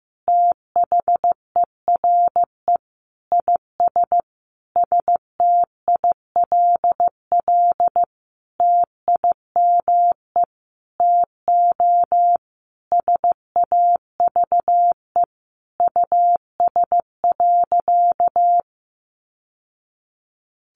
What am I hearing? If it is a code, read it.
Morse code, heard as a keyed tone, about 15 wpm, 700 Hz. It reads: THERE IS STILL TIME TO SAVE US.
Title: Sentences from Top 1100 Words 15wpm